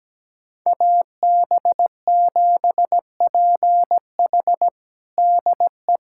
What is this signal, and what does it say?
Morse code, heard as a keyed tone, about 17 wpm, 700 Hz. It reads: AB7PH DE
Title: USA Callsign DE Callsign - 5 Characters - 17wpm